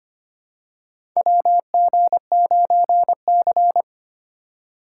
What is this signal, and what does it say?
Morse code, heard as a keyed tone, about 25 wpm, 700 Hz. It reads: WG9C
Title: USA Call Signs - 4 Characters - 4x Character Spacing - 25wpm